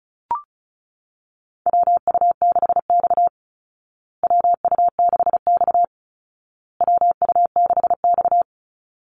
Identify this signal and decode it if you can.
Morse code, heard as a keyed tone, about 34 wpm, 700 Hz. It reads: WU6X WU6X WU6X
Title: Call Signs Repeated 2x - 35wpm